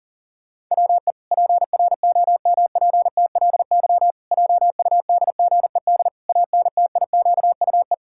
Morse code, heard as a keyed tone, about 40 wpm, 700 Hz. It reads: WE PROMPTLY JUDGED ANTIQUE
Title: Pangram Challenge v2 - 40wpm